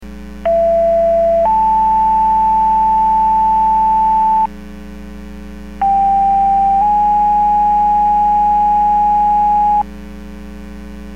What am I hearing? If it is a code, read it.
Two-tone page, heard as a one-second tone, then a three-second tone, about 670 then 880 Hz, about 790 then 830 Hz. It is heard from 0.5 to 9.8 s.